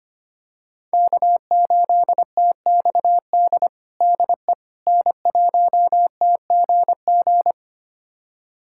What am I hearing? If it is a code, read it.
Morse code, heard as a keyed tone, about 25 wpm, 700 Hz. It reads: K8TXD DE N1TGG
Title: USA Callsign DE Callsign - 5 Characters - 25wpm